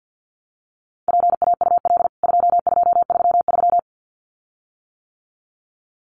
Morse code, heard as a keyed tone, about 50 wpm, 700 Hz. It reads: PAUL 2233
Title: CWT Exchange 50wpm